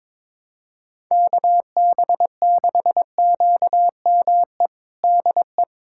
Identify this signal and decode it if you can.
Morse code, heard as a keyed tone, about 22 wpm, 700 Hz. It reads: KB6QME DE
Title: USA Callsign DE Callsign - 6 Characters - 22wpm